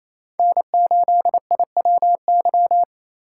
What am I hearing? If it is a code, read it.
Morse code, heard as a keyed tone, about 36 wpm, 700 Hz. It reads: N8IWY